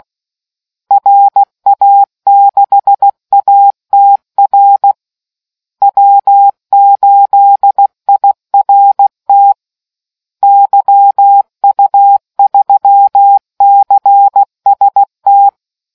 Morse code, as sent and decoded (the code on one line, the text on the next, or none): .-. .- -.... .- - .-. / .-- ---.. .. .-. - / -.-- ..- ...-- -.-. ... -
RA6ATR W8IRT YU3CST